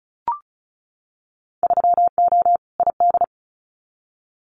Morse code, heard as a keyed tone, about 36 wpm, 700 Hz. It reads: 3O ID
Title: Winter Field Day Exchanges 35wpm